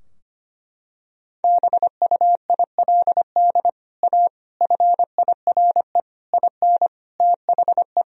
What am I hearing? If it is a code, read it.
Morse code, heard as a keyed tone, about 25 wpm, 700 Hz. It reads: BUILD A FIRE IN THE
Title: Sentences from Top 1000 Words - Review Part 4 - 25wpm